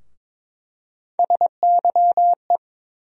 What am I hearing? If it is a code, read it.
Morse code, heard as a keyed tone, about 22 wpm, 700 Hz. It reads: SYE
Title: USA Call Sign Postfixes - 3 Letters 22wpm